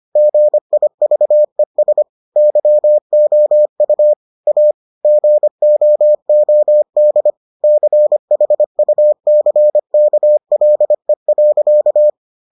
Morse code, heard as a keyed tone, about 25 wpm, 600 Hz. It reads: GIVES YOU A GOOD CHUCKLE.